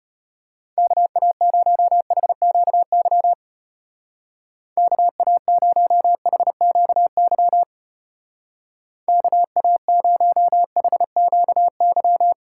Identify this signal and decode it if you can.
Morse code, heard as a keyed tone, about 31 wpm, 700 Hz. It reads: KA0HQY KA0HQY KA0HQY
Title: Speed Racing - USA 2x3 Call Signs 25wpm